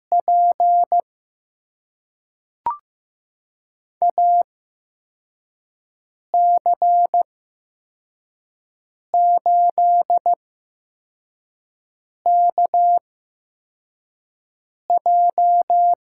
Morse code, heard as a keyed tone, about 15 wpm, spaced as for 4 wpm, 700 Hz. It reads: P AC8KJ
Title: USA Call Signs - 5 Characters - 8x Character Spacing - 15wpm